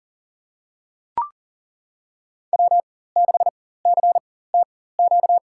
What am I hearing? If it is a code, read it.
Morse code, heard as a keyed tone, about 40 wpm, spaced as for 19 wpm, 700 Hz. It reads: W6CTQ
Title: USA Call Signs - 5 Characters - 4x Character Spacing - 40wpm